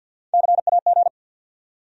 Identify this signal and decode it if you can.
Morse code, heard as a keyed tone, about 48 wpm, 690 Hz. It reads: CAG
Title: USA Call Sign Postfixes - 3 Letters 50wpm